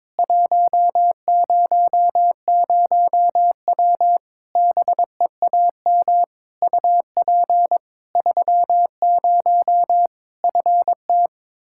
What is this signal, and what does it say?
Morse code, heard as a keyed tone, about 22 wpm, 700 Hz. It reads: 100W BEAM UP 30 FT
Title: Practice QSO - Exchange 2 - Info - 22wpm